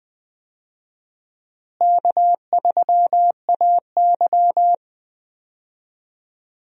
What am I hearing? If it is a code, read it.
Morse code, heard as a keyed tone, about 20 wpm, 700 Hz. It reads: K3AY
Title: USA Call Signs - 4 Characters - Repeated 2x - 20wpm